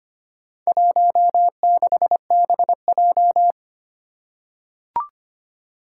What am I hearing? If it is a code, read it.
Morse code, heard as a keyed tone, about 25 wpm, 700 Hz. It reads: 16BJ
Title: Four Challenging Letter-Number Combinations 25wpm